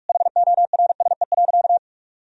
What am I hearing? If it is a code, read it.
Morse code, heard as a keyed tone, about 46 wpm, 700 Hz. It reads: HORSE.
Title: Sentences from Top 1000 Words - Review Part 1 - 45wpm